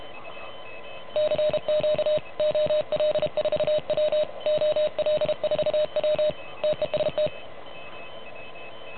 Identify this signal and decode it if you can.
Morse code, heard as a keyed tone, about 33 wpm, 610 Hz. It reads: CQ OL4W OL4W TEST
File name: truSDX / MWC29.8